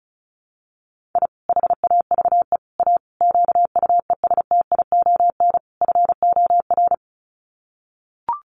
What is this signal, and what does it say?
Morse code, heard as a keyed tone, about 34 wpm, 700 Hz. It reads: I HAVE A QUESTION FOR
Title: Sets of 5 Words Encore2 35wpm